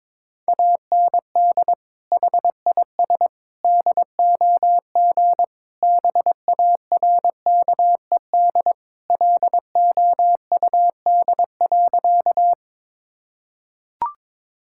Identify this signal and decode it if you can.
Morse code, heard as a keyed tone, about 22 wpm, 700 Hz. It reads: AND HIS DOG BARKED LOUD.
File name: practice / 22wpm / Sentences with 2-Word Phrases - 22wpm